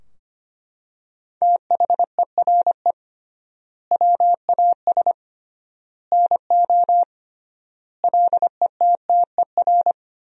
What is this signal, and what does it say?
Morse code, heard as a keyed tone, about 25 wpm, 700 Hz. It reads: THERE WAS NO LETTER
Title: 3x Word-Spacing - Sentences from Top 300 Words - 25wpm